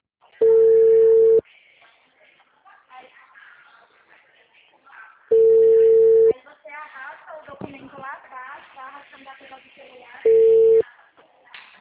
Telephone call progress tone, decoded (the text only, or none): ringback tone